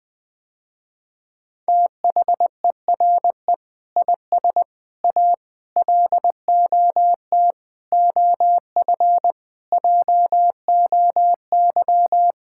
Morse code, heard as a keyed tone, about 20 wpm, 700 Hz. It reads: THERE IS A LOT OF JOY
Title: Sentences with 5-Word Phrases - Spoken Beginning - 20wpm